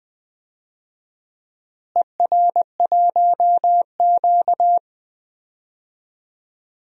Morse code, heard as a keyed tone, about 20 wpm, 700 Hz. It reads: ER1Q
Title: DXCC-100 Call Signs - 4 Characters - Repeated 2x - 20wpm